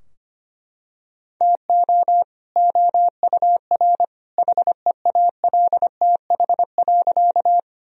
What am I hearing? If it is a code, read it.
Morse code, heard as a keyed tone, about 25 wpm, 700 Hz. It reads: TO OUR HEALTH.
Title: Sentences from Top 1000 Words - Review Part 3 - 25wpm